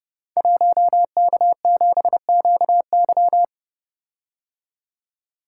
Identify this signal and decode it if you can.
Morse code, heard as a keyed tone, about 29 wpm, 700 Hz. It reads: 1K7QY
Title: Code Groups - 5 Letter:Numbers - 30wpm